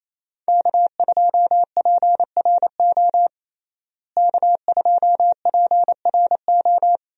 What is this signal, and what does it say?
Morse code, heard as a keyed tone, about 28 wpm, 700 Hz. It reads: K2PRO K2PRO
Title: USA Call Signs - 5 Characters - Repeated 3x - 28wpm